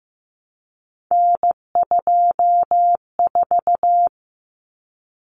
Morse code, heard as a keyed tone, about 15 wpm, 700 Hz. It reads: N24
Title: Common Cut Numbers - 3-Digit 15wpm